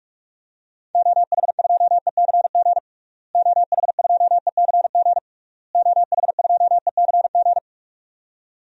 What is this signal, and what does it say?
Morse code, heard as a keyed tone, about 46 wpm, 700 Hz. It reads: OH2ECG OH2ECG OH2ECG
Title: DXCC-100 Call Signs - 6 Characters - Repeated 3x - 45wpm